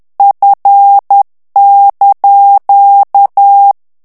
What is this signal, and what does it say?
Morse code, heard as a keyed tone, about 10 wpm, 800 Hz. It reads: F)
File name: ITALIEN texte clair 2 vitesse 10 mots minute